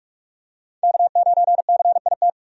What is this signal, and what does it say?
Morse code, heard as a keyed tone, about 44 wpm, 700 Hz. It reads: K9CIT
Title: USA Call Signs - 5 Characters - 45wpm